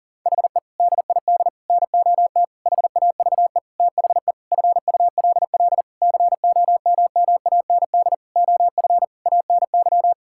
Morse code, heard as a keyed tone, about 40 wpm, 700 Hz. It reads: HE DID NOT HAVE THE FULL COMMAND OF ANY
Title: Sentences from Top 1600 Words 40wpm